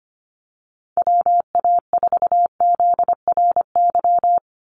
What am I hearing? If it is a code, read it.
Morse code, heard as a keyed tone, about 25 wpm, 690 Hz. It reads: WA4ZRY
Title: USA Call Signs 25wpm